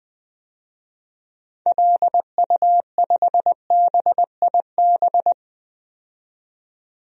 Morse code, heard as a keyed tone, about 20 wpm, 700 Hz. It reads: LU5BIB
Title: DXCC-100 Call Signs - 6 Characters - 20wpm